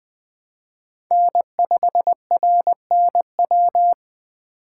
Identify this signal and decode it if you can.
Morse code, heard as a keyed tone, about 20 wpm, 700 Hz. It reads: N5RNW